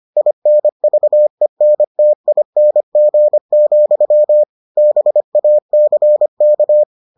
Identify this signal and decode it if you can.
Morse code, heard as a keyed tone, about 25 wpm, 600 Hz. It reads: INVENTING, BACK